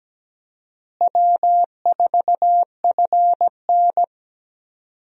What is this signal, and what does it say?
Morse code, heard as a keyed tone, about 17 wpm, 700 Hz. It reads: W4FN
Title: USA Call Signs - 4 Characters - Repeated 2x - 17wpm